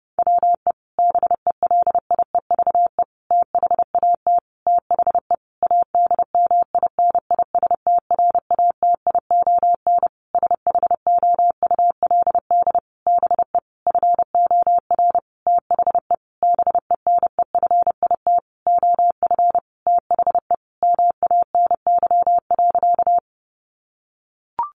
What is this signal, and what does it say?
Morse code, heard as a keyed tone, about 28 wpm, 700 Hz. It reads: WE BELIEVE THAT THE ADMINISTRATION SHOULD BE FOR THE BENEFIT OF THE MANY.
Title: Sentences from Top 1300 Words 30wpm